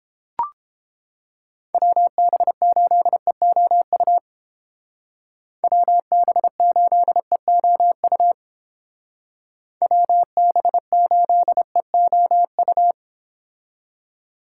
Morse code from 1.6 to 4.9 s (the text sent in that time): WB8EOU